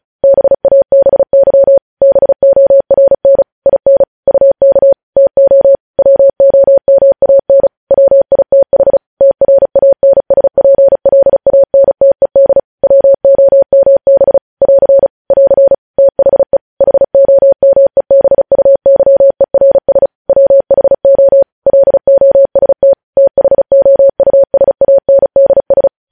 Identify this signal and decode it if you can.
Morse code, heard as a keyed tone, about 36 wpm, 550 Hz. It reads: BABY BORN IN UK TO WOMAN WITH TRANSPLANTED WOMB <AR> <AR> THE HOMEBUYERS WHO LOST THOUSANDS
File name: podcast / News headlines-1x-35x35